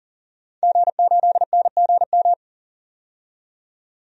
Morse code, heard as a keyed tone, about 40 wpm, 700 Hz. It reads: G8NGM
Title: DXCC-100 Call Signs - 5 Characters - 40wpm